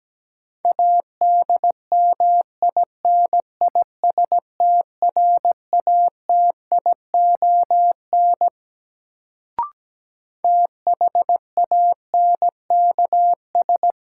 Morse code, heard as a keyed tone, about 17 wpm, 700 Hz. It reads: ADMINISTRATION THANKS
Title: Sets of 5 Words - New Word Drill - 17wpm